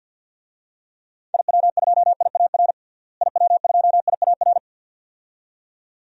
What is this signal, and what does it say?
Morse code, heard as a keyed tone, about 50 wpm, 700 Hz. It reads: IW2IAR IW2IAR
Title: DXCC-100 Call Signs - 6 Characters - Repeated 2x - 50wpm